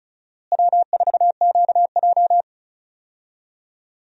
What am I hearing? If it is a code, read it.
Morse code, heard as a keyed tone, about 36 wpm, 700 Hz. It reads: W4QJ
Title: USA Call Signs - 4 Characters - Repeated 3x - 35wpm